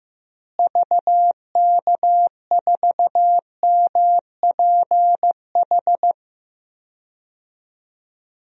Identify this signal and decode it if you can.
Morse code, heard as a keyed tone, about 16 wpm, 700 Hz. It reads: VK4MPH
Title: DXCC-100 Call Signs - 6 Characters - Repeated 3x - 15wpm